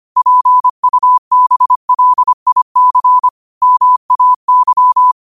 Morse code, heard as a keyed tone, about 25 wpm, 1000 Hz. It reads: PUBLIC MAY
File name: qotdfiles / 25wpm